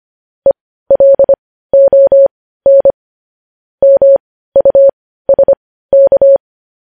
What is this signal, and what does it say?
Morse code, heard as a keyed tone, about 25 wpm, spaced as for 15 wpm, 550 Hz. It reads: ELON MUSK
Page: news headlines in morse code at 15 wpm